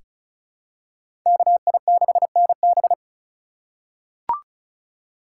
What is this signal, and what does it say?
Morse code, heard as a keyed tone, about 36 wpm, 700 Hz. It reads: KI6NB
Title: USA Call Signs - 5 Characters - 2x Character Spacing - 35wpm